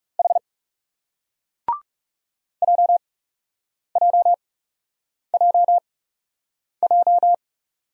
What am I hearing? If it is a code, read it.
Morse code, heard as a keyed tone, about 36 wpm, spaced as for 8 wpm, 700 Hz. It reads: H JJJJ
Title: Speed-Racing Single Letters 30wpm